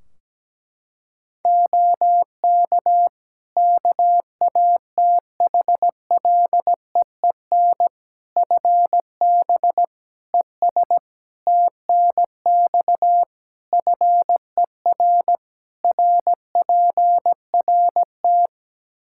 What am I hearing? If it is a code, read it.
Morse code, heard as a keyed tone, about 17 wpm, 700 Hz. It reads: OK KATHLEEN FB ES TNX FER RPRT
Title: Practice QSO - Exchange 2 - Info - 17wpm